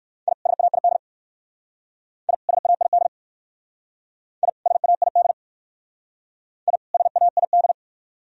Morse code, heard as a keyed tone, about 54 wpm, 700 Hz. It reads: I SAID I SAID I SAID I SAID
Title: Speed-Racing - Sets of 2 Words 45wpm